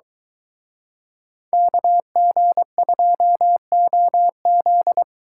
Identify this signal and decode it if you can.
Morse code, heard as a keyed tone, about 23 wpm, 700 Hz. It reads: KG2OZ